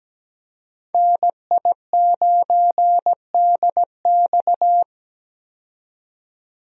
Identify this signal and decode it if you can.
Morse code, heard as a keyed tone, about 17 wpm, 700 Hz. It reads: NI9DX
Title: USA Call Signs - 5 Characters - 17wpm